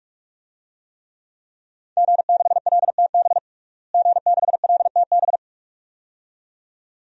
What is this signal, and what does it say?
Morse code, heard as a keyed tone, about 46 wpm, 700 Hz. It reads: G6LTB G6LTB
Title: DXCC-100 Call Signs - 5 Characters - Repeated 2x - 45wpm